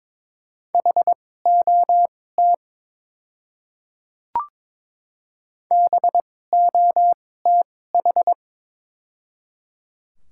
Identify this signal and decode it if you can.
Morse code, heard as a keyed tone, about 22 wpm, spaced as for 16 wpm, 700 Hz. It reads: HOT BOTH
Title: 2x Character Spacing - Top 300 Words - 22wpm